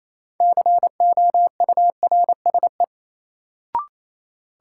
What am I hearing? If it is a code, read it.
Morse code, heard as a keyed tone, about 28 wpm, 700 Hz. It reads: COURSE
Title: Sentences from Top 500 Words - New Word Drill - 28wpm